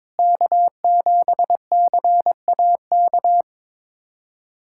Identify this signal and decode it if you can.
Morse code, heard as a keyed tone, about 21 wpm, 700 Hz. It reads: K7CAK
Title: USA Callsign DE Callsign - 5 Characters - 22wpm